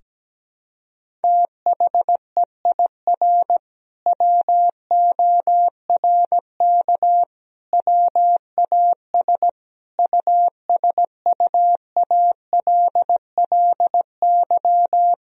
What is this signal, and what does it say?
Morse code, heard as a keyed tone, about 17 wpm, 700 Hz. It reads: THEIR WORK WAS USUALLY
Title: Sentences from Top 1000 Words - Review Part 3 - 17wpm